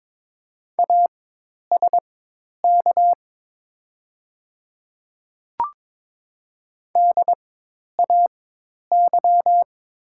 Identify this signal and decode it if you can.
Morse code, heard as a keyed tone, about 22 wpm, spaced as for 10 wpm, 700 Hz. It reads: ASK DAY
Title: 4x Character Spacing - Top 200 Words - 22wpm